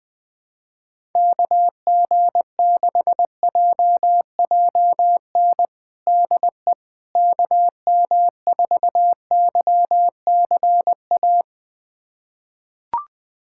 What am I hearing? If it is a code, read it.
Morse code, heard as a keyed tone, about 21 wpm, 700 Hz. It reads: KG6JJN DE KM4YCA
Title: USA Callsign DE Callsign - 6 Characters - 20wpm